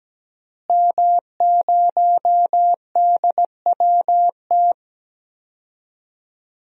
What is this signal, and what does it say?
Morse code, heard as a keyed tone, about 17 wpm, 700 Hz. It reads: M0DWT
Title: DXCC-100 Call Signs - 5 Characters - Repeated 3x - 17wpm